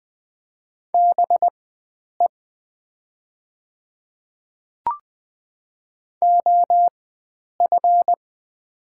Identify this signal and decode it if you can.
Morse code, heard as a keyed tone, about 20 wpm, spaced as for 9 wpm, 700 Hz. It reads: BE OF
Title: 4x Character Spacing - Top 100 Words - 20wpm